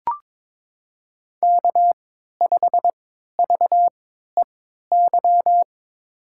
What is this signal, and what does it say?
Morse code, heard as a keyed tone, about 22 wpm, spaced as for 13 wpm, 700 Hz. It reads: K5VEY
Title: USA Call Signs - 5 Characters - 3x Character Spacing - 22wpm